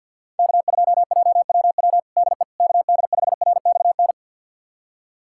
Morse code, heard as a keyed tone, about 51 wpm, 700 Hz. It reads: K2JWW DE KD5RXN
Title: USA Callsign DE Callsign 50wpm